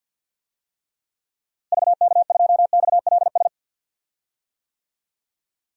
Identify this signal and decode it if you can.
Morse code, heard as a keyed tone, about 52 wpm, 700 Hz. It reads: VK2XLS
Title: DXCC-100 Call Signs - 6 Characters - Repeated 3x - 50wpm